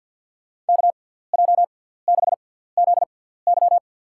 Morse code, heard as a keyed tone, about 52 wpm, spaced as for 17 wpm, 700 Hz. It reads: KJ6ZY
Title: USA Call Signs - 5 Characters - 6x Character Spacing - 50wpm